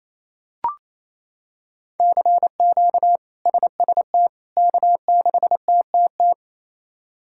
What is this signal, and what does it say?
Morse code, heard as a keyed tone, about 28 wpm, 700 Hz. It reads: CQ SST K6TTT
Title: K1USN SST QSO Practice - 28wpm